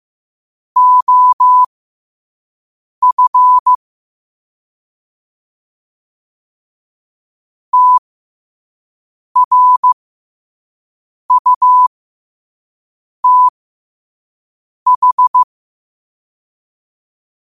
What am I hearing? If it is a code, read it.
Morse code, heard as a keyed tone, about 15 wpm, spaced as for 5 wpm, 1000 Hz. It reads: OF TRUTH
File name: qotdfiles / 5wpm